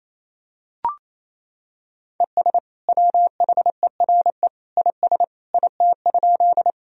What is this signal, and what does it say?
Morse code, heard as a keyed tone, about 28 wpm, 700 Hz. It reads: ES WHERE IS IT?
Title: Sentences from Top 200 Words and Abbreviations 28wpm